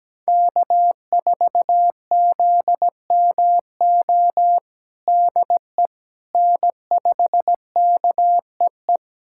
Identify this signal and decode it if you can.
Morse code, heard as a keyed tone, about 17 wpm, 700 Hz. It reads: K4ZMO DE N5KEE
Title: USA Callsign DE Callsign - 5 Characters - 17wpm